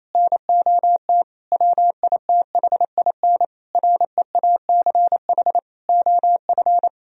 Morse code, heard as a keyed tone, about 28 wpm, 700 Hz. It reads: NOT WITHIN REACH OF
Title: Sentences from Top 1000 Words - Review Part 4 - 28wpm